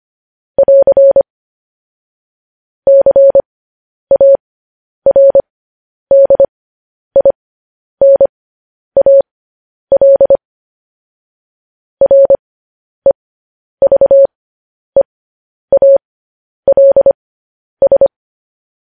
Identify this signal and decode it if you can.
Morse code, heard as a keyed tone, about 25 wpm, spaced as for 10 wpm, 550 Hz. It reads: <AR> CARDINAL REVEALS
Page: News Headlines in Morse Code at 10 WPM